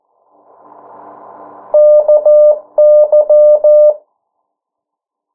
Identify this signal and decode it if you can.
Morse code, heard as a keyed tone, about 14 wpm, 600 Hz. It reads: KY
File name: morse mp3 / 09 urok 8